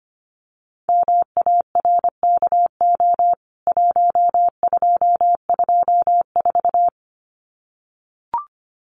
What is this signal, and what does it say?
Morse code, heard as a keyed tone, about 25 wpm, 700 Hz. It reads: MARKO 1224
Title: CWT Exchange 25wpm